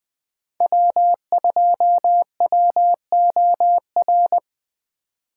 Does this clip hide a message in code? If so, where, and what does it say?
Yes: Morse code from 0.6 to 4.4 s, heard as a keyed tone, about 20 wpm, 700 Hz. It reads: W2WOR